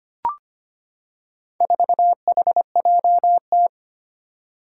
Morse code, heard as a keyed tone, about 25 wpm, 700 Hz. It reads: 4HJT
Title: Four Challenging Letter-Number Combinations 25wpm